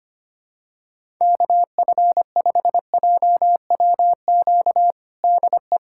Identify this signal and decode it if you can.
Morse code, heard as a keyed tone, about 25 wpm, 700 Hz. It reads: KF5JWQ DE
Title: USA Callsign DE Callsign - 6 Characters - 25wpm